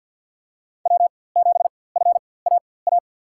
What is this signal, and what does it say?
Morse code, heard as a keyed tone, about 48 wpm, spaced as for 23 wpm, 700 Hz. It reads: W7FAA
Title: USA Call Signs - 5 Characters - 4x Character Spacing - 50wpm